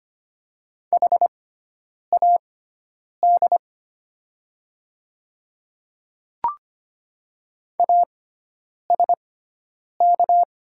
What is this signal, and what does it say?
Morse code, heard as a keyed tone, about 25 wpm, spaced as for 9 wpm, 700 Hz. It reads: HAD ASK
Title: 6x Character Spacing - Top 200 Words - 25wpm